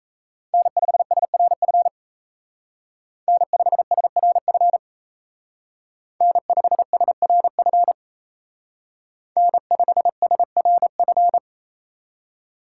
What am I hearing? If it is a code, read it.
Morse code, heard as a keyed tone, about 35 wpm, 700 Hz. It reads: N5SRF N5SRF N5SRF N5SRF
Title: Speed Racing - USA Call Signs 28wpm